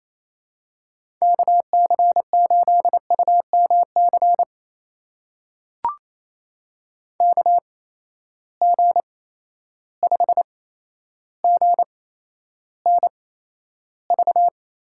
Morse code, heard as a keyed tone, about 28 wpm, spaced as for 12 wpm, 700 Hz. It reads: KC8UMC KG5GNV
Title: USA Call Signs - 6 Characters - 8x Character Spacing - 28wpm